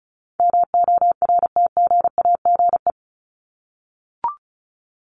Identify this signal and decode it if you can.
Morse code, heard as a keyed tone, about 34 wpm, 700 Hz. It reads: MORTGAGE
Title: Top 1751-2000 Words 35wpm